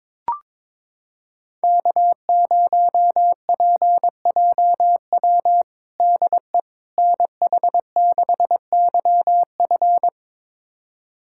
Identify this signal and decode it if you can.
Morse code, heard as a keyed tone, about 22 wpm, 700 Hz. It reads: K0PJW DE NH6YF
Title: USA Callsign DE Callsign - 5 Characters - 22wpm